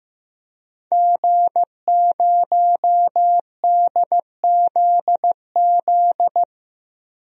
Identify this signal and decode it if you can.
Morse code, heard as a keyed tone, about 15 wpm, 700 Hz. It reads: G0DZZ
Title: DXCC-100 Call Signs - 5 Characters - Repeated 3x - 15wpm